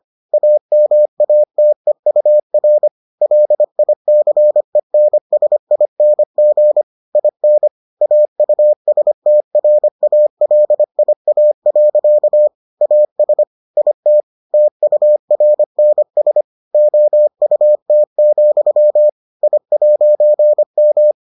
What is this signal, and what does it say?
Morse code, heard as a keyed tone, about 25 wpm, 600 Hz. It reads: AMATEUR LICENSING IN AUSTRALIA. AS IT TURNS OUT, I'M